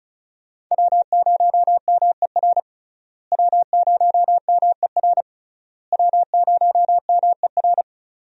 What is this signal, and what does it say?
Morse code, heard as a keyed tone, about 36 wpm, 700 Hz. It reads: W0MER W0MER W0MER
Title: USA Call Signs - 5 Characters - Repeated 3x - 35wpm